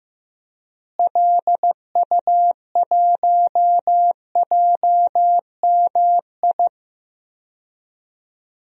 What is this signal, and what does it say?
Morse code, heard as a keyed tone, about 15 wpm, 700 Hz. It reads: LU1JMI
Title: DXCC-100 Call Signs - 6 Characters - Repeated 2x - 15wpm